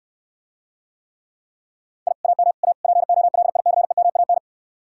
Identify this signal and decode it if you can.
Morse code, heard as a keyed tone, about 69 wpm, 700 Hz. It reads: I AM A POLICEMAN